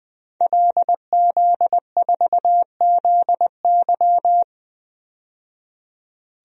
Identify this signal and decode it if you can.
Morse code, heard as a keyed tone, about 21 wpm, 700 Hz. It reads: LZ4ZY
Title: DXCC-100 Call Signs - 5 Characters - 20wpm